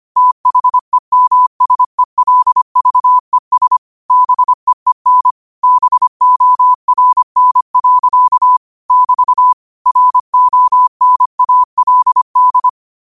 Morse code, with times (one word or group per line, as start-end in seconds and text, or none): 0.2-3.8 THEMSELVES
4.1-5.3 BEEN
5.6-8.6 BORN.
8.9-9.5 <BT>
9.9-12.7 RONALD